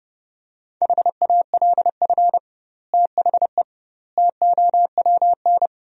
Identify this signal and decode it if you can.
Morse code, heard as a keyed tone, about 30 wpm, 700 Hz. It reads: HALF THE TOWN
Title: 2x Word-Spacing - Sentences from Top 500 Words - 30wpm